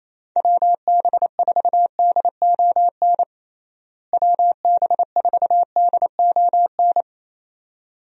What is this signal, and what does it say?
Morse code, heard as a keyed tone, about 28 wpm, 700 Hz. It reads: WB4DON WB4DON
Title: USA Call Signs - 6 Characters - Repeated 2x - 28wpm